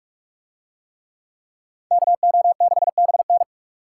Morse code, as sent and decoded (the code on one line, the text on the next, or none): -.- --- -.... -... -.
KO6BN